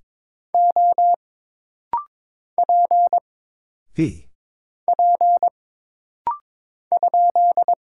Morse code, heard as a keyed tone, about 22 wpm, 700 Hz. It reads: O P P ?
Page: This rapid-fire timing is geared to push your ICR (Instant Character Recognition) skills.